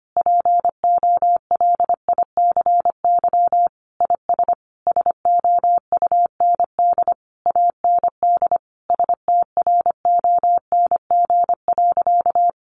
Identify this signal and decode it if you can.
Morse code, heard as a keyed tone, about 25 wpm, 700 Hz. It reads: POLICY IS SOUND AND STRONG.
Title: Sentences from Top 1400 Words 25wpm